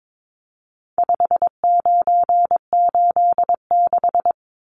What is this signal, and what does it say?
Morse code, heard as a keyed tone, about 22 wpm, 700 Hz. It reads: 5986
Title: Numbers - 4-Digit 22wpm